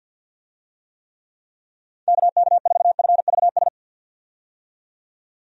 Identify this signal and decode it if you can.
Morse code, heard as a keyed tone, about 50 wpm, 700 Hz. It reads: KK4FVS
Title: USA Call Signs - 6 Characters - 50wpm